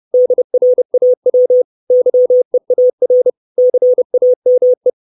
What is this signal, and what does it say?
Morse code, heard as a keyed tone, about 30 wpm, 500 Hz. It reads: DRAW YEAR CAME
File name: 632 Words 30WPM